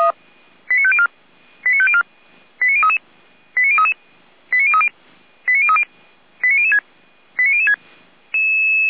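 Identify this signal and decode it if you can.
Five-tone selective call (ZVEI), heard as quick runs of five tones, tones of about 70 ms. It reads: 89504 89504 89033 89033 89030 89030 89006 89006
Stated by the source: ZVEI five-tone call received by SDR, from feuerwehr-orlamuende.org fire brigade audio archive.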